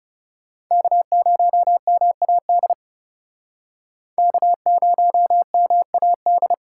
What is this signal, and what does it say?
Morse code, heard as a keyed tone, about 37 wpm, 700 Hz. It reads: K0MAD K0MAD
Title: Speed Racing - USA 1x3 Call Signs 30wpm